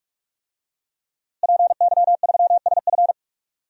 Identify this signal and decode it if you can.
Morse code, heard as a keyed tone, about 47 wpm, 700 Hz. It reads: PY3SF
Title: DXCC-100 Call Signs - 5 Characters - Repeated 2x - 45wpm